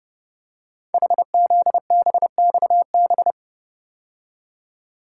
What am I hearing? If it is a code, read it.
Morse code, heard as a keyed tone, about 30 wpm, 700 Hz. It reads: HZBXB
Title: Code Groups - 5 Letters - 30wpm